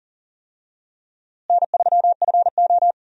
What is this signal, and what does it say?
Morse code, heard as a keyed tone, about 40 wpm, 700 Hz. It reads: N3FO
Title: USA Call Signs - 4 Characters - Repeated 2x - 40wpm